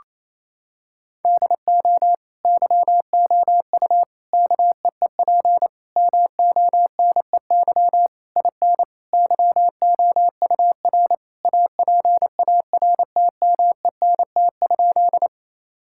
Morse code, heard as a keyed tone, about 28 wpm, 700 Hz. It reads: DO YOU KEEP MONEY IN YOUR APARTMENT?
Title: Sentences from Top 1000 Words - Review Part 6 - 28wpm